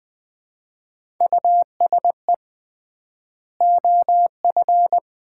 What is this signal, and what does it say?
Morse code, heard as a keyed tone, about 20 wpm, 700 Hz. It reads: USE OF
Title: 3x Word-Spacing - Sets of 2 Words - 20wpm